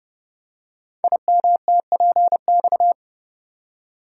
Morse code, heard as a keyed tone, about 30 wpm, 700 Hz. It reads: IMTPX
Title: Code Groups - 5 Letters - 30wpm